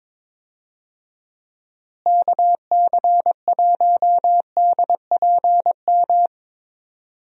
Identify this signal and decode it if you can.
Morse code, heard as a keyed tone, about 22 wpm, 700 Hz. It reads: KC1DPM
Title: USA Call Signs - 6 Characters - Repeated 3x - 22wpm